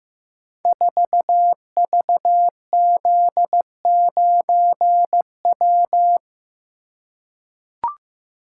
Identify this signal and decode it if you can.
Morse code, heard as a keyed tone, about 15 wpm, 700 Hz. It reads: 4VZ9W
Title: Code Groups - 5 Letter:Numbers - 15wpm